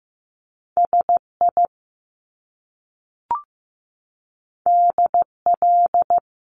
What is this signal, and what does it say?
Morse code, heard as a keyed tone, about 15 wpm, 700 Hz. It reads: SI DL
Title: Top Two Letter Combinations in Words 15wpm